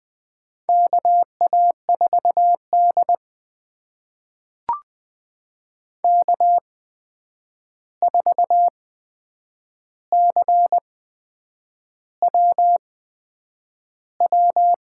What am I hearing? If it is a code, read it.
Morse code, heard as a keyed tone, about 20 wpm, spaced as for 8 wpm, 700 Hz. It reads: KA4D K4CWW
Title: USA Call Signs - 8x Character Spacing - 20wpm